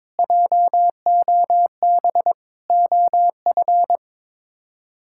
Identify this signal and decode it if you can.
Morse code, heard as a keyed tone, about 22 wpm, 700 Hz. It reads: JOB OF
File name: practice / 22wpm / Sets of 7 Words - 22wpm